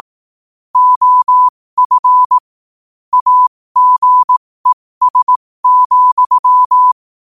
Morse code, heard as a keyed tone, about 18 wpm, spaced as for 15 wpm, 1000 Hz. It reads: OF AGES,
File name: qotdfiles / 15wpm